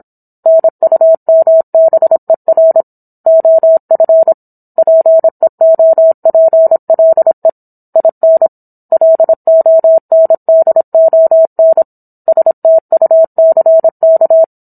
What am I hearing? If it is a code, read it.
Morse code, heard as a keyed tone, about 27 wpm, 650 Hz. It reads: NUMBER OF PEOPLE IN LONDON STUCK
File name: Book One / 17-Chapter 17 - 26 WPM